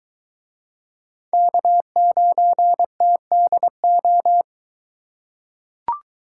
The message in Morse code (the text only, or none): K9TDO